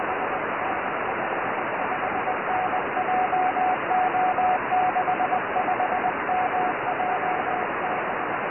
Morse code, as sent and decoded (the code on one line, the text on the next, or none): none